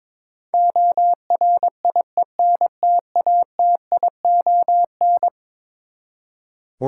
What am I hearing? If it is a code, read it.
Morse code, heard as a keyed tone, about 22 wpm, 700 Hz. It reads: ORIENTATION
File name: practice / 22wpm / Eleven Letter Words - 22wpm